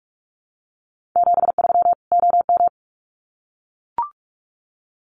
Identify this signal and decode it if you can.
Morse code, heard as a keyed tone, about 44 wpm, 700 Hz. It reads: 73 OM
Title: Common QSO Phrases 45wpm